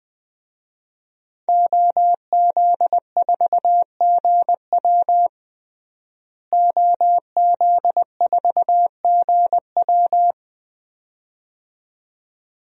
Morse code, heard as a keyed tone, about 19 wpm, 700 Hz. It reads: OZ4GW OZ4GW
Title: DXCC-100 Call Signs - 5 Characters - Repeated 2x - 20wpm